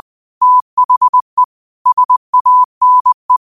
Morse code, heard as a keyed tone, about 20 wpm, 1000 Hz. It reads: THE SANE
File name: 20wpm